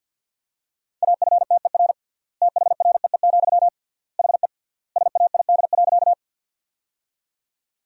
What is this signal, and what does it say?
Morse code, heard as a keyed tone, about 50 wpm, 700 Hz. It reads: AFTER THREE, HE SAID.
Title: 3x Word-Spacing - Sentences from Top 200 Words - 50wpm